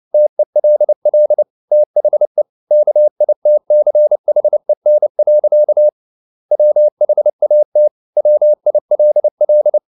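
Morse code, heard as a keyed tone, about 29 wpm, 600 Hz. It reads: TELL THE KITCHEN. WHAT WILL
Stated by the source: Morse code at 29 wpm, sending Ken Follett's novel Never in English, from CW med Gnister.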